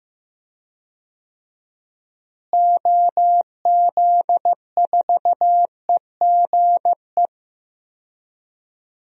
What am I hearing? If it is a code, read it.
Morse code, heard as a keyed tone, about 15 wpm, 700 Hz. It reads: OZ4EGE
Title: DXCC-100 Call Signs - 6 Characters - Repeated 3x - 15wpm